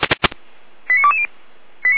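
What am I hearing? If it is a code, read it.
Five-tone selective call (ZVEI), heard as quick runs of five tones, tones of about 80 ms. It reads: 89220
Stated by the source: ZVEI five-tone selective call, received by SDR.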